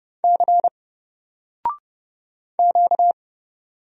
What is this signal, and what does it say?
Morse code, heard as a keyed tone, about 30 wpm, 700 Hz. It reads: C Q
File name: Single-Letter-Number-30wpm